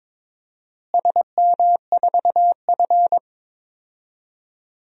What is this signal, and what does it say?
Morse code, heard as a keyed tone, about 21 wpm, 700 Hz. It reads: SM4F
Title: DXCC-100 Call Signs - 4 Characters - Repeated 3x - 22wpm